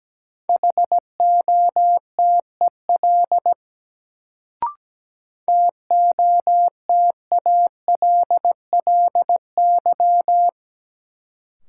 Morse code, heard as a keyed tone, about 17 wpm, 700 Hz. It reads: HOTEL TOTALLY
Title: Sentences from Top 1600 Words - New Word Drill Alt - 17wpm